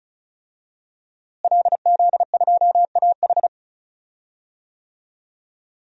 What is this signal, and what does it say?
Morse code, heard as a keyed tone, about 36 wpm, 700 Hz. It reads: LZ2AH
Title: DXCC-100 Call Signs - 5 Characters - Repeated 2x - 35wpm